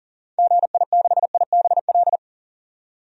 Morse code, heard as a keyed tone, about 40 wpm, 700 Hz. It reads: GI6IBL
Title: DXCC-100 Call Signs - 6 Characters - 40wpm